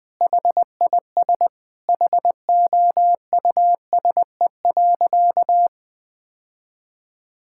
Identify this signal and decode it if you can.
Morse code, heard as a keyed tone, about 19 wpm, 700 Hz. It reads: HIS HOUSE.
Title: Sentences with 2-Word Phrases - Spoken Beginning - 20wpm